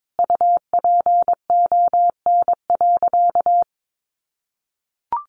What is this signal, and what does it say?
Morse code, heard as a keyed tone, about 22 wpm, 700 Hz. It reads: UPON.
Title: Sentences from Top 1400 Words 22wpm